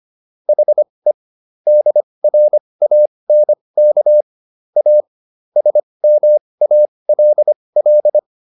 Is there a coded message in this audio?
Yes: Morse code from 0.5 to 8.2 s, heard as a keyed tone, about 25 wpm, spaced as for 20 wpm, 600 Hz. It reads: HE DRANK A SMALL